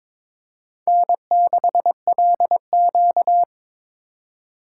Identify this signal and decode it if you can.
Morse code, heard as a keyed tone, about 22 wpm, 690 Hz. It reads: N6LQ
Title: USA Call Signs - 4 Characters - 3x Character Spacing - 22wpm